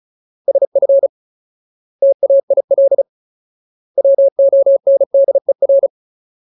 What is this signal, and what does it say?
Morse code, heard as a keyed tone, about 34 wpm, 550 Hz. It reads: SF TAIL WONDER
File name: Ultimate Word List / 35WPM Ultimate